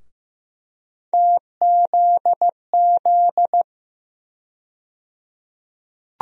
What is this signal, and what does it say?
Morse code, heard as a keyed tone, about 15 wpm, 700 Hz. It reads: TZZ
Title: USA Call Sign Postfixes - 3 Letters 15wpm